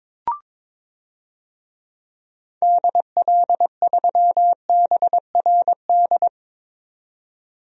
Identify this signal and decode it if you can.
Morse code, heard as a keyed tone, about 22 wpm, 700 Hz. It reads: DL3BRD
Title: DXCC-100 Call Signs - 6 Characters - 22wpm